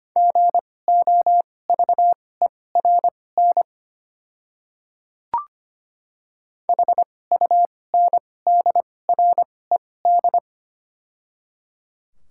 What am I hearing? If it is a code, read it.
Morse code, heard as a keyed tone, about 25 wpm, spaced as for 18 wpm, 700 Hz. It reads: GOVERN HUNDRED
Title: 2x Character Spacing - Top 500 Words - 25wpm